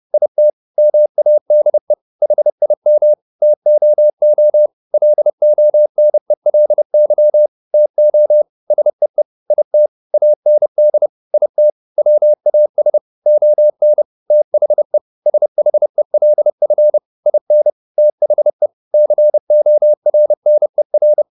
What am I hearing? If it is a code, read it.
Morse code, heard as a keyed tone, about 30 wpm, 600 Hz. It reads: IT MADE HIM TOO LONELY TO SEE IT AND IT WAS ON THE SHELF IN THE CORNER
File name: the old man and the sea / the old man and the sea 1 30wpm